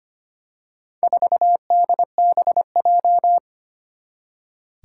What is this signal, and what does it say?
Morse code, heard as a keyed tone, about 25 wpm, 700 Hz. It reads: 4DBJ
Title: Four Challenging Letter-Number Combinations 25wpm